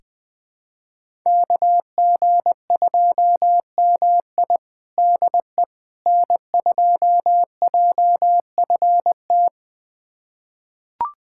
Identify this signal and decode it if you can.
Morse code, heard as a keyed tone, about 20 wpm, 700 Hz. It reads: KG2MI DE N2JFT
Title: USA Callsign DE Callsign 20wpm